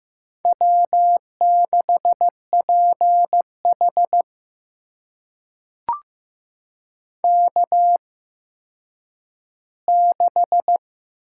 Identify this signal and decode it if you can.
Morse code, heard as a keyed tone, about 15 wpm, spaced as for 9 wpm, 700 Hz. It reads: W6PH K6
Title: USA Call Signs - 8x Character Spacing - 15wpm